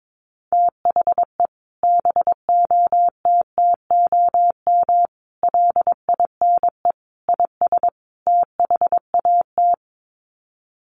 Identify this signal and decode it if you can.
Morse code, heard as a keyed tone, about 22 wpm, 700 Hz. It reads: THE BOTTOM LINE IS THAT
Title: Sets of 5 Words Encore 22wpm